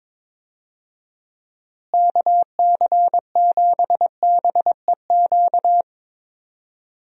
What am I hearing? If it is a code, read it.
Morse code, heard as a keyed tone, about 22 wpm, 700 Hz. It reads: KC7BEQ
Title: USA Call Signs - 6 Characters - Repeated 2x - 22wpm